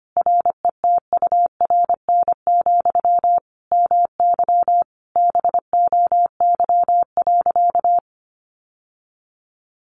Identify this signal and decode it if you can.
Morse code, heard as a keyed tone, about 25 wpm, 700 Hz. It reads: RETURN, MY BOY.
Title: Sentences from Top 1000 Words 25wpm